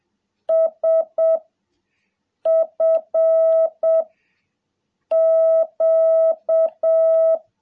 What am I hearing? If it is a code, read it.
Morse code, heard as a keyed tone, about 7 wpm, 660 Hz. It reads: S F Q
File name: jh8cbh / 2021cw